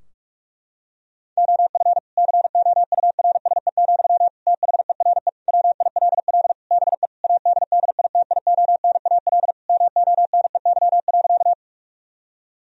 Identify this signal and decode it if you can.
Morse code, heard as a keyed tone, about 46 wpm, 700 Hz. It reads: OF COURSE, THERE WILL BE ADDITIONAL MONEY.
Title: Sentences from Top 1000 Words - Review Part 4 - 45wpm